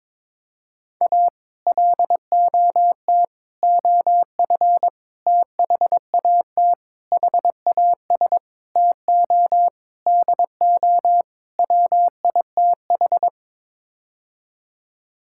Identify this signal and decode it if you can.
Morse code, heard as a keyed tone, about 22 wpm, 700 Hz. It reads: A LOT OF THAT HAS TO DO WITH
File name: practice / 22wpm / Sets of 8 Words - 22wpm